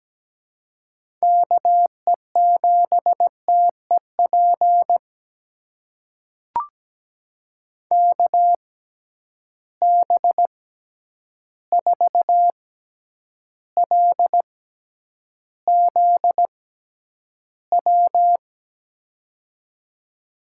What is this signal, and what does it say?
Morse code, heard as a keyed tone, about 17 wpm, spaced as for 9 wpm, 700 Hz. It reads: KE7TEP KB4LZW
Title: USA Call Signs - 6 Characters - 6x Character Spacing - 17wpm